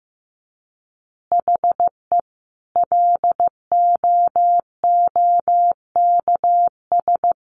Morse code, heard as a keyed tone, about 15 wpm, 700 Hz. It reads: HE LOOKS
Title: Additional Easy Phrases 15wpm